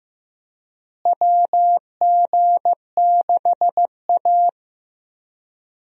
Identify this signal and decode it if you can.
Morse code, heard as a keyed tone, about 15 wpm, 700 Hz. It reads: WG6A
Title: USA Call Signs - 4 Characters - Repeated 2x - 15wpm